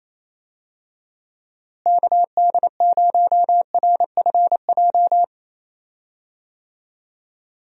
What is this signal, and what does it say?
Morse code, heard as a keyed tone, about 28 wpm, 700 Hz. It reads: KD0RFJ
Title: USA Call Signs - 6 Characters - Repeated 3x - 28wpm